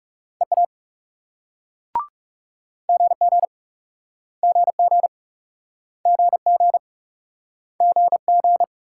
Morse code, heard as a keyed tone, about 38 wpm, 700 Hz. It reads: EA GG GG GG GG